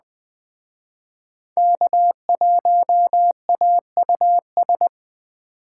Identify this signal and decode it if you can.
Morse code, heard as a keyed tone, about 20 wpm, 700 Hz. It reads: K1AUS